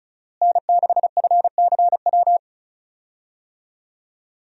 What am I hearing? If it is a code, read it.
Morse code, heard as a keyed tone, about 34 wpm, 700 Hz. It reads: N6FCW
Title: USA Call Signs - 5 Characters - Repeated 3x - 35wpm